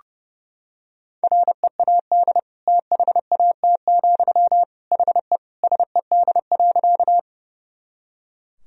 Morse code, heard as a keyed tone, about 32 wpm, 700 Hz. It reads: READ THAT, HE SED.
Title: Sentences from Top 200 Words and Abbreviations 30wpm